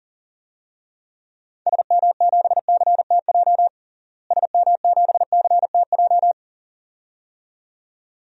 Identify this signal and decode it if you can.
Morse code, heard as a keyed tone, about 40 wpm, 700 Hz. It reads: SM7CTJ SM7CTJ
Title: DXCC-100 Call Signs - 6 Characters - Repeated 2x - 40wpm